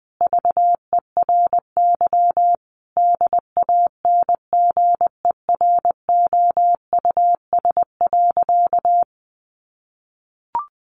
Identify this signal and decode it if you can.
Morse code, heard as a keyed tone, about 19 wpm, 700 Hz. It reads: VERY DANGEROUS.
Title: Sentences from Top 1200 Words 20wpm